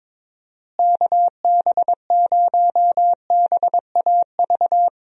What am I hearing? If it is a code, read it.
Morse code, heard as a keyed tone, about 22 wpm, 700 Hz. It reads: KB0BAV
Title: USA Call Signs - 6 Characters - Repeated 2x - 22wpm